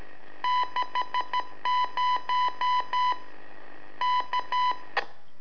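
Morse code, heard as a keyed tone, about 13 wpm, 1000 Hz. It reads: 60 K
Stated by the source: Telemetry in Morse from LU7AA and LU4EZ, AMSAT and Turma Mar del Plata, sent from a balloon.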